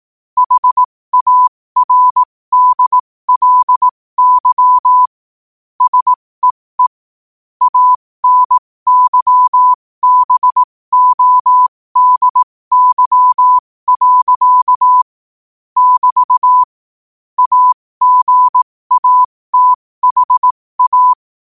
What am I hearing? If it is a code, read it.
Morse code, heard as a keyed tone, about 18 wpm, spaced as for 15 wpm, 1000 Hz. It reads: HARDLY SEE ANYBODY. <BT> AGATHA
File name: qotdfiles / 15wpm